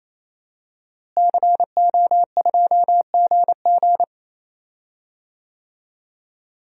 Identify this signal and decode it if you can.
Morse code, heard as a keyed tone, about 28 wpm, 700 Hz. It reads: CO2GG
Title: DXCC-100 Call Signs - 5 Characters - Repeated 3x - 28wpm